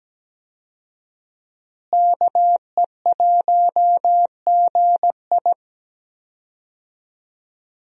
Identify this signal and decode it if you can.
Morse code, heard as a keyed tone, about 17 wpm, 700 Hz. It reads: KE1GI
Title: USA Call Signs - 5 Characters - Repeated 2x - 17wpm